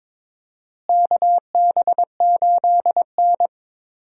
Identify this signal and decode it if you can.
Morse code, heard as a keyed tone, about 22 wpm, 700 Hz. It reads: KB8N
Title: USA Call Signs - 4 Characters - Repeated 3x - 22wpm